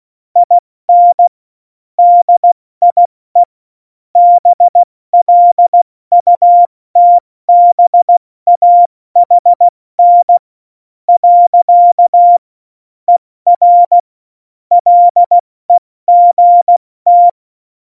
Morse code, heard as a keyed tone, about 16 wpm, spaced as for 13 wpm, 700 Hz. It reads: IN DIE BLUTBAHN. ER LEGT
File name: Morsekurs / Witze / Witz 08. Dezember 2020, 70 F80